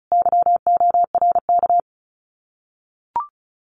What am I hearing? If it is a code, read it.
Morse code, heard as a keyed tone, about 36 wpm, 700 Hz. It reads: YORK
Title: Common QSO Phrases Encore 35wpm